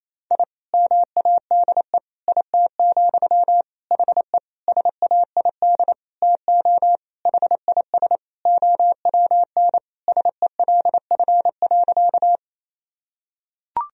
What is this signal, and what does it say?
Morse code, heard as a keyed tone, about 28 wpm, 700 Hz. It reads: I MADE IT, HE SAID TO HIS OWN SELF.
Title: Sentences from Top 1000 Words - Review Part 1 - 28wpm